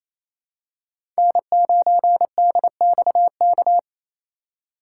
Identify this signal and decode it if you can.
Morse code, heard as a keyed tone, about 28 wpm, 700 Hz. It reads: N9DXK
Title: USA Call Signs - 5 Characters - 4x Character Spacing - 28wpm